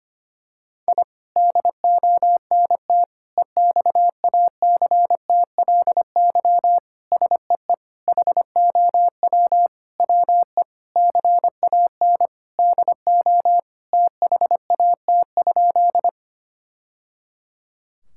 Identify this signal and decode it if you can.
Morse code, heard as a keyed tone, about 25 wpm, 700 Hz. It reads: I DONT EXACTLY SEE HOW WE CAN DO THAT?
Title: Sentences from Top 1000 Words - Review Part 6 - 25wpm